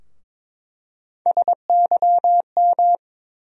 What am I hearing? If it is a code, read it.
Morse code, heard as a keyed tone, about 22 wpm, 700 Hz. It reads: SYM
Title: USA Call Sign Postfixes - 3 Letters 22wpm